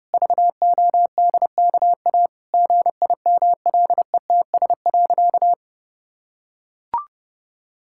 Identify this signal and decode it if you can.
Morse code, heard as a keyed tone, about 32 wpm, 700 Hz. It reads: VODKA GIMLETS.
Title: Pangram Challenge v2 - 30wpm